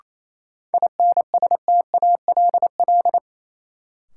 Morse code, heard as a keyed tone, about 28 wpm, 700 Hz. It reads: INSTALL
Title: Sentences from Top 1600 Words - New Word Drill Alt - 28wpm